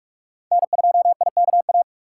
Morse code, heard as a keyed tone, about 46 wpm, 700 Hz. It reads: N2IKA
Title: USA Call Signs - 6x Character Spacing - 45wpm